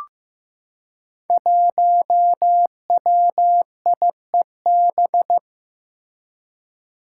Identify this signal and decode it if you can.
Morse code, heard as a keyed tone, about 15 wpm, 700 Hz. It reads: 1WIEB
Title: Five Challenging Letter-Number Combinations 15wpm